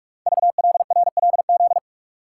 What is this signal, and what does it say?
Morse code, heard as a keyed tone, about 44 wpm, 700 Hz. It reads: VLRLZ
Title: Code Groups - 5 Letters - 45wpm